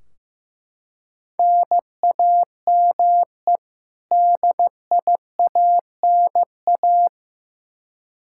Morse code, heard as a keyed tone, about 16 wpm, 700 Hz. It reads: NAME DIANA
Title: Practice QSO - Exchange 1 - Beginning - 15wpm